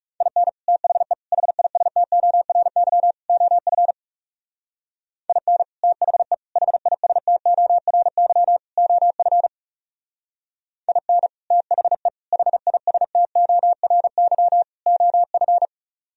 Morse code, heard as a keyed tone, about 40 wpm, 700 Hz. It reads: IN THE HISTORY OF IN THE HISTORY OF IN THE HISTORY OF